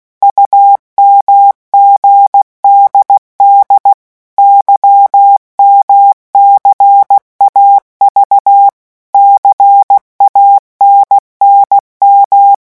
Morse code, heard as a keyed tone, about 16 wpm, 800 Hz. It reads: UMGDD YMCAV CANNM
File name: morse mp3 / revision3-vitesse 15 mots